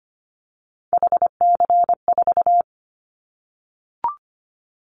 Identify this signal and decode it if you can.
Morse code, heard as a keyed tone, about 25 wpm, 700 Hz. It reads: HC4
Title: Callsign Prefixes for Top 100 DXCC - 25wpm